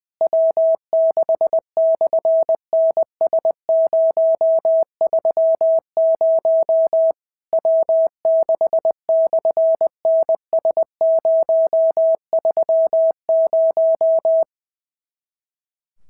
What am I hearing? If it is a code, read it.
Morse code, heard as a keyed tone, about 20 wpm, 650 Hz. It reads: W6/NS030 W6/NS030
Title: SOTA QSO Practice 20wpm